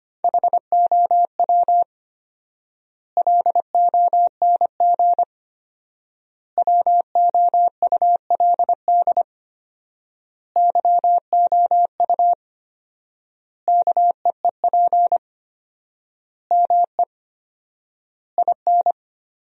Morse code, heard as a keyed tone, about 25 wpm, 700 Hz. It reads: HOW LONG WOULD YOU KEEP ME IN
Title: 4x Word-Spacing - Sentences from Top 500 Words - 25wpm